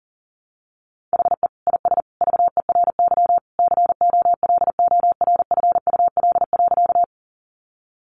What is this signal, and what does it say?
Morse code, heard as a keyed tone, about 40 wpm, 700 Hz. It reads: HE IS VERY COLORFUL.
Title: Easy Phrases 40wpm